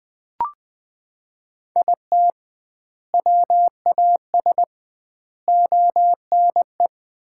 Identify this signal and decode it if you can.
Morse code, heard as a keyed tone, about 20 wpm, 700 Hz. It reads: IT WAS ONE
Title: 2x Word-Spacing - Sets of 4 Words - 20wpm